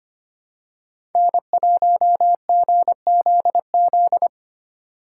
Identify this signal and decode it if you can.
Morse code, heard as a keyed tone, about 25 wpm, 700 Hz. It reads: N1GZZ
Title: USA Call Signs - 8x Character Spacing - 25wpm